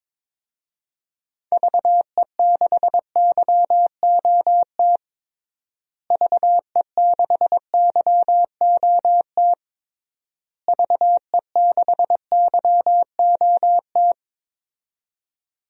Morse code, heard as a keyed tone, about 22 wpm, 700 Hz. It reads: VE6YOT VE6YOT VE6YOT
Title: DXCC-100 Call Signs - 6 Characters - Repeated 3x - 22wpm